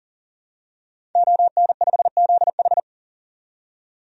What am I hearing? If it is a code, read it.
Morse code, heard as a keyed tone, about 39 wpm, 700 Hz. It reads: ON5ZH
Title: DXCC-100 Call Signs - 5 Characters - 40wpm